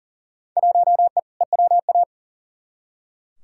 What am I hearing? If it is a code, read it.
Morse code, heard as a keyed tone, about 40 wpm, 700 Hz. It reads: 1E EWA
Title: Field Day Exchanges - 40wpm